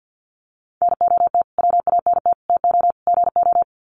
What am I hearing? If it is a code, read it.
Morse code, heard as a keyed tone, about 53 wpm, 700 Hz. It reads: NOT WANT TO GO
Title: Sets of 4 Words 50wpm